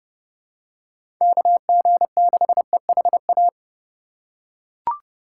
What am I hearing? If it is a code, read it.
Morse code, heard as a keyed tone, about 32 wpm, 700 Hz. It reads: KG6EHA
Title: USA Call Signs - 6 Characters - 2x Character Spacing - 30wpm